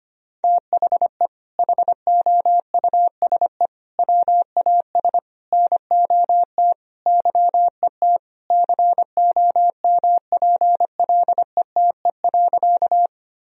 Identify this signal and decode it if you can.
Morse code, heard as a keyed tone, about 25 wpm, 700 Hz. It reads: THE HOUSE WAS NOT YET COMPLETE.
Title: Sentences from Top 1000 Words - Review Part 2 - 25wpm